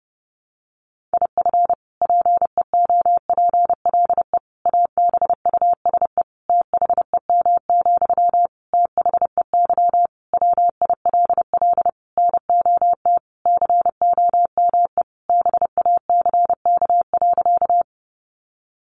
Morse code, heard as a keyed tone, about 30 wpm, 700 Hz. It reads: IF PEOPLE ABUSE THEM, THEY WILL NOT COME BACK.
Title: Sentences from Top 1300 Words 30wpm